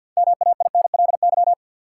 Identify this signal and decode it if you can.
Morse code, heard as a keyed tone, about 48 wpm, 700 Hz. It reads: MAINLY
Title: Sentences from Top 1400 Words - New Word Drill Alt - 50wpm